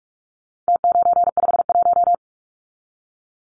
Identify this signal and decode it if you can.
Morse code, heard as a keyed tone, about 44 wpm, 700 Hz. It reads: T951
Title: Numbers - Contest Style - 4-Digit 45wpm